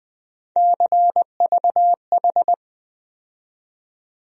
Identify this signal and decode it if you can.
Morse code, heard as a keyed tone, about 20 wpm, 700 Hz. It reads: CVH
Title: USA Call Sign Postfixes - 3 Letters 20wpm